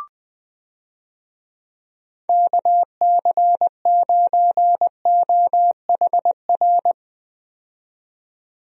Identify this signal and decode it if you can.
Morse code, heard as a keyed tone, about 21 wpm, 700 Hz. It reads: KC9OHR
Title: USA Call Signs - 6 Characters - 20wpm